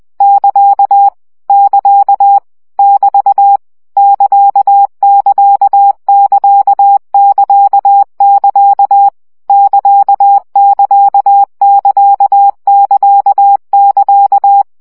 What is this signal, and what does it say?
Morse code, heard as a keyed tone, about 20 wpm, 800 Hz. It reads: <KA> <KA> <BT> <KA><KA><KA><KA><KA> <KA><KA><KA><KA><KA>
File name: CW gyakorlas ct 17 baud